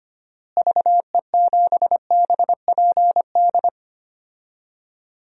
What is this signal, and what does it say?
Morse code, heard as a keyed tone, about 25 wpm, 700 Hz. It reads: VE7BPD
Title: DXCC-100 Call Signs - 6 Characters - Repeated 3x - 25wpm